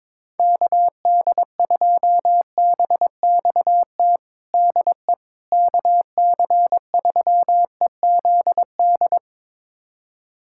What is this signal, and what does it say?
Morse code, heard as a keyed tone, about 22 wpm, 700 Hz. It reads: KD2BXT DE KC3EZD
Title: USA Callsign DE Callsign 22wpm